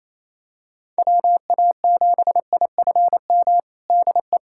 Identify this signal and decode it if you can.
Morse code, heard as a keyed tone, about 28 wpm, 700 Hz. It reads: WA7IFM DE
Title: USA Callsign DE Callsign - 6 Characters - 28wpm